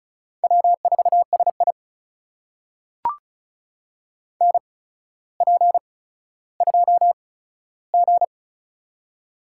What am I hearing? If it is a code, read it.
Morse code, heard as a keyed tone, about 36 wpm, spaced as for 15 wpm, 700 Hz. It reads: W4SI NP2G
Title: USA Call Signs - 4 Characters - 8x Character Spacing - 35wpm